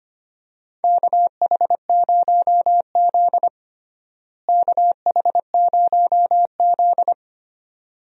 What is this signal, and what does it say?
Morse code, heard as a keyed tone, about 25 wpm, 700 Hz. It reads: KH0Z KH0Z
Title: USA Call Signs - 4 Characters - Repeated 2x - 25wpm